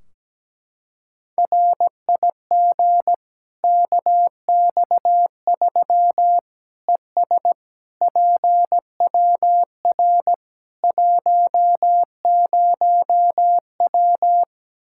Morse code, heard as a keyed tone, about 17 wpm, 700 Hz. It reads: RIG KX3 ES PWR 10W
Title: Practice QSO - Exchange 2 - Info - 17wpm